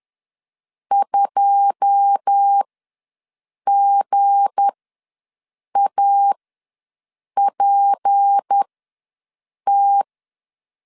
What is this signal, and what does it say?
Morse code, heard as a keyed tone, about 10 wpm, spaced as for 6 wpm, 800 Hz. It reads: 2GAPT
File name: LEZIONI / 10 SIGNS / 10D COMA ws 10wpm 4314